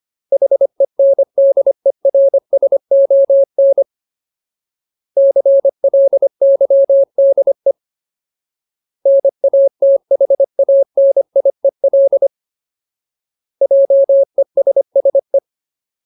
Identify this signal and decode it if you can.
Morse code, heard as a keyed tone, about 25 wpm, 550 Hz. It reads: HENDERSON CLYDE NATHANIEL JESSE
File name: Ultimate Word List / 25WPM Ultimate